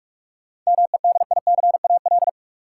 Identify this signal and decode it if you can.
Morse code, heard as a keyed tone, about 43 wpm, 700 Hz. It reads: MEDICAL
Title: Sentences from Top 800 Words - New Word Drill - 45wpm